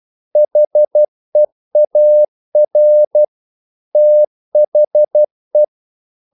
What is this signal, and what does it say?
Morse code, heard as a keyed tone, about 12 wpm, 600 Hz. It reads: HEAR THE
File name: CW / Practice / If12